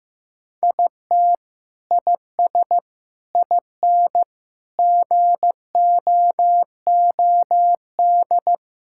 Morse code, heard as a keyed tone, about 15 wpm, 700 Hz. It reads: IT IS IN GOOD
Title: Sentences from Top 1000 Words - Review Part 6 - 15wpm